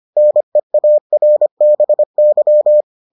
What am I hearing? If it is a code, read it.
Morse code, heard as a keyed tone, about 25 wpm, 600 Hz. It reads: NEARBY